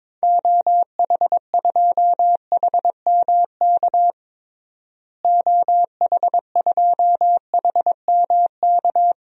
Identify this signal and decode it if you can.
Morse code, heard as a keyed tone, about 21 wpm, 700 Hz. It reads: OH2HMK OH2HMK
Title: DXCC-100 Call Signs - 6 Characters - Repeated 3x - 22wpm